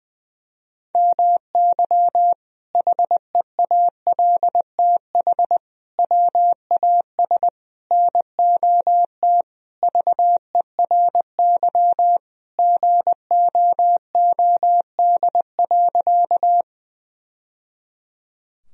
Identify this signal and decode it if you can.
Morse code, heard as a keyed tone, about 20 wpm, 700 Hz. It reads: MY HEALTH WAS NOT VERY GOOD.
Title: Sentences from Top 1000 Words - Review Part 3 - 20wpm